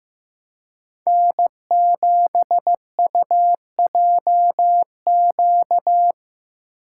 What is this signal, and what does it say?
Morse code, heard as a keyed tone, about 15 wpm, 700 Hz. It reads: N7UJQ
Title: USA Call Signs - 5 Characters - 3x Character Spacing - 15wpm